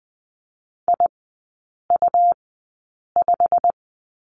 Morse code, heard as a keyed tone, about 20 wpm, spaced as for 8 wpm, 700 Hz. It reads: IU5
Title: 127 Lesson 21e - Number 6 - Three Characters 20wpm